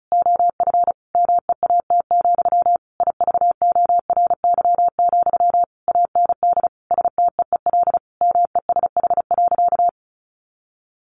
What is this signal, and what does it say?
Morse code, heard as a keyed tone, about 34 wpm, 700 Hz. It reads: OF MEAT, IVORY, AND STEEL MESH.
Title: A Sound Of Thunder 35wpm